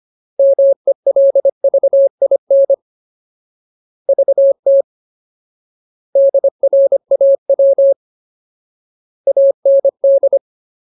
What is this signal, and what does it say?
Morse code, heard as a keyed tone, about 25 wpm, 550 Hz. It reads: MELVIN VT DRAW AND
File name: Ultimate Word List / 25WPM Ultimate